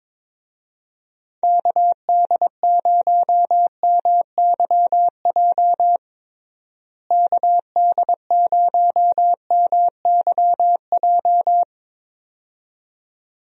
Morse code, heard as a keyed tone, about 22 wpm, 700 Hz. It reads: KD0MYJ KD0MYJ
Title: USA Call Signs - 6 Characters - Repeated 2x - 22wpm